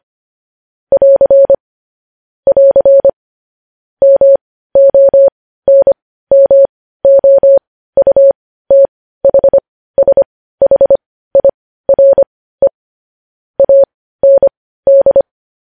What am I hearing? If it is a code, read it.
Morse code, heard as a keyed tone, about 25 wpm, spaced as for 15 wpm, 550 Hz. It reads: <AR> <AR> MONMOUTHSHIRE AND
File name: podcast / News headlines-1x-25x15